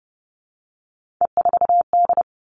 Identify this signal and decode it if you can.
Morse code, heard as a keyed tone, about 30 wpm, 700 Hz. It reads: E4D
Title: Three Challenging Letter-Number Combinations 30wpm